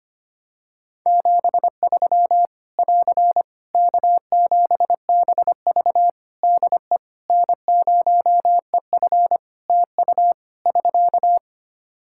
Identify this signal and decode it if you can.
Morse code, heard as a keyed tone, about 25 wpm, 700 Hz. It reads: 73 <AR> K7BV DE N0EF TU <SK>
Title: Practice QSO - Exchange 4 - Ending - 25wpm